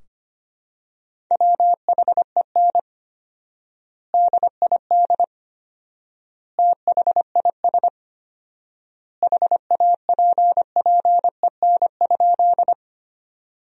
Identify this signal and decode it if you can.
Morse code, heard as a keyed tone, about 25 wpm, 700 Hz. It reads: WHEN DID THIS HAPPEN?
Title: 4x Word-Spacing - Sentences from Top 400 Words - 25wpm